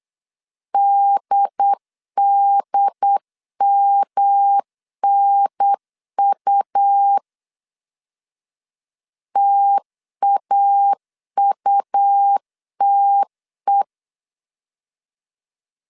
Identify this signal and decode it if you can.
Morse code, heard as a keyed tone, about 8 wpm, 800 Hz. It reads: DDMNU TAUTE
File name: CW / DET / 1 3 1500C S B 8wpm 4141